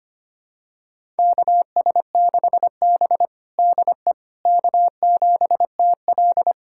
Morse code, heard as a keyed tone, about 25 wpm, 700 Hz. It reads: KS6B DE K7TL
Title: USA Callsign DE Callsign - 4 Characters - 25wpm